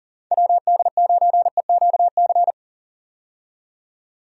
Morse code, heard as a keyed tone, about 40 wpm, 700 Hz. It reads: WD9EQC
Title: USA Call Signs - 6 Characters - 40wpm